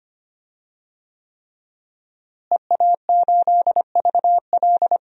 Morse code, heard as a keyed tone, about 25 wpm, 700 Hz. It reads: EA8VL
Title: DXCC-100 Call Signs - 5 Characters - Repeated 3x - 25wpm